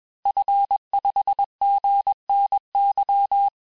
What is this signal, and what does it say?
Morse code, heard as a keyed tone, about 21 wpm, 800 Hz. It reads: F5GNY
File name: teste-FIN-de-QSO-vitesse-20mpm